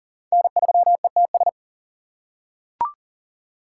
Morse code, heard as a keyed tone, about 40 wpm, 700 Hz. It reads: N3ETS
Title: USA Call Signs - 5 Characters - 4x Character Spacing - 40wpm